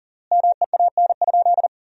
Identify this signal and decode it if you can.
Morse code, heard as a keyed tone, about 40 wpm, 700 Hz. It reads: MEAN?
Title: 3x Word-Spacing - Sentences from Top 200 Words - 40wpm